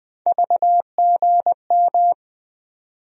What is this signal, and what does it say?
Morse code, heard as a keyed tone, about 20 wpm, 700 Hz. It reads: VGM
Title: USA Call Sign Postfixes - 3 Letters 20wpm